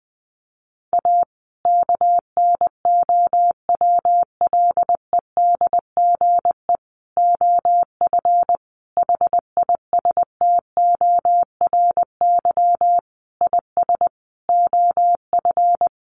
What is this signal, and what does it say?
Morse code, heard as a keyed tone, about 20 wpm, 700 Hz. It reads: A KNOWLEDGE OF HISTORY IS OF
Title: Sentences from Top 900 Words 20wpm